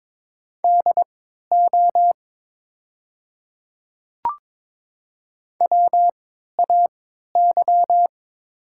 Morse code, heard as a keyed tone, about 22 wpm, spaced as for 12 wpm, 690 Hz. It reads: DO WAY
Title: 3x Character Spacing - Top 200 Words - 22wpm